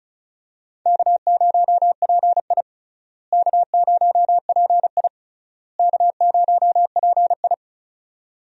Morse code, heard as a keyed tone, about 36 wpm, 700 Hz. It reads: K0PI K0PI K0PI
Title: USA Call Signs - 4 Characters - Repeated 3x - 35wpm